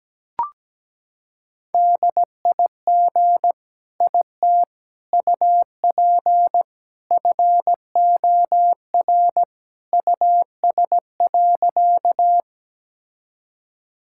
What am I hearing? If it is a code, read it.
Morse code, heard as a keyed tone, about 17 wpm, 700 Hz. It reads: DIG IT UP FOR US.
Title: Sentences from Top 1600 Words 17wpm